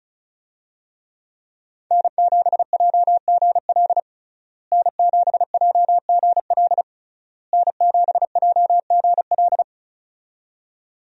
Morse code, heard as a keyed tone, about 36 wpm, 700 Hz. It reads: N7JGL N7JGL N7JGL
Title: USA Call Signs - 5 Characters - Repeated 3x - 35wpm